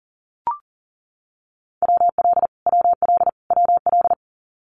Morse code, heard as a keyed tone, about 40 wpm, 700 Hz. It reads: WL WL WL
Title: Abbreviations Repeated 3x - 40wpm